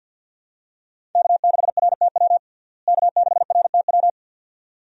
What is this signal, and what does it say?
Morse code, heard as a keyed tone, about 48 wpm, 700 Hz. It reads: K6RTW K6RTW
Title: USA Call Signs - 5 Characters - Repeated 2x - 50wpm